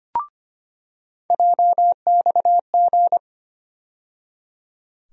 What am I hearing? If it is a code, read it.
Morse code, heard as a keyed tone, about 25 wpm, 700 Hz. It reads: JXG
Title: USA Call Sign Postfixes - 3 Letters 25wpm